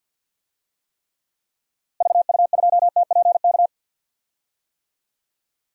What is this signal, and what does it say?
Morse code, heard as a keyed tone, about 52 wpm, 700 Hz. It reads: VU2TPK
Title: DXCC-100 Call Signs - 6 Characters - Repeated 3x - 50wpm